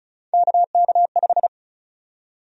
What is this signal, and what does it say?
Morse code, heard as a keyed tone, about 36 wpm, 700 Hz. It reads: KK5
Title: USA Call Sign Prefixes - 35wpm